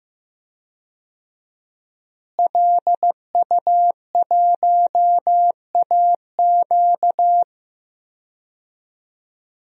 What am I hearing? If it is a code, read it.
Morse code, heard as a keyed tone, about 15 wpm, 700 Hz. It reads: LU1AQ
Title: DXCC-100 Call Signs - 5 Characters - Repeated 2x - 15wpm